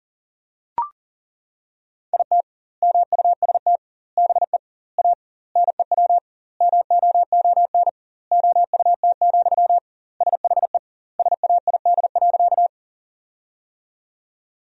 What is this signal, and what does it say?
Morse code, heard as a keyed tone, about 40 wpm, 700 Hz. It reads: IT MUST BE A NEW MOON OUT, SHE SAID.
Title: 2x Word-Spacing - Sentences from Top 500 Words - 40wpm